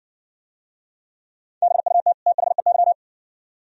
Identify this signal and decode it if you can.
Morse code, heard as a keyed tone, about 57 wpm, 700 Hz. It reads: BUT THEY